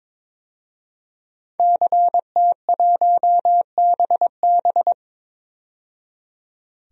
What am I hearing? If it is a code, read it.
Morse code, heard as a keyed tone, about 22 wpm, 700 Hz. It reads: CT1BB
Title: DXCC-100 Call Signs - 5 Characters - 22wpm